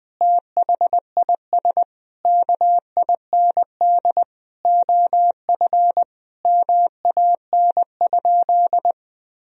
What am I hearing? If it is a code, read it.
Morse code, heard as a keyed tone, about 20 wpm, 700 Hz. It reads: THIS KIND OF MAN?
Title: Sentences from Top 1000 Words - Review Part 4 - 20wpm